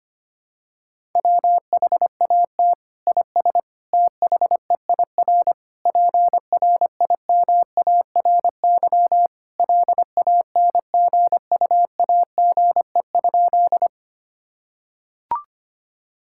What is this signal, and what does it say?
Morse code, heard as a keyed tone, about 25 wpm, 700 Hz. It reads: WHAT IS THEIR PRIMARY LANGUAGE?
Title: Sentences from Top 1000 Words - Review Part 5 - 25wpm